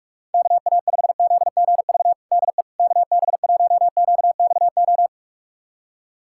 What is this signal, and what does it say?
Morse code, heard as a keyed tone, about 45 wpm, 700 Hz. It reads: KA5ZGV DE KB1QXO
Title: USA Callsign DE Callsign - 6 Characters - 45wpm